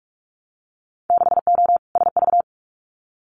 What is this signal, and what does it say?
Morse code, heard as a keyed tone, about 46 wpm, 700 Hz. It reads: 6O SV
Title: Winter Field Day Exchanges Encore 45wpm